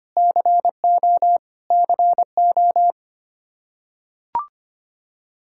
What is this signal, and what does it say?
Morse code, heard as a keyed tone, about 25 wpm, 700 Hz. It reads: CO CO
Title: Practice QSO - Exchange 1 - Beginning - 25wpm